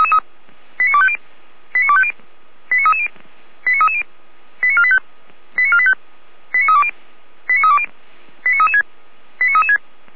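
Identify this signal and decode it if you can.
Five-tone selective call (ZVEI), heard as quick runs of five tones, tones of about 70 ms. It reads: 89250 89250 89330 89330 89485 89485 89320 89320 89336 89336
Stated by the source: ZVEI five-tone selective call received by SDR, channel 455.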